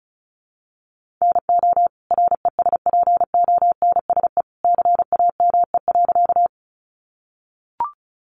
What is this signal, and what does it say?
Morse code, heard as a keyed tone, about 34 wpm, 700 Hz. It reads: NO RESPONSE CAME.
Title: Sentences from Top 1400 Words 35wpm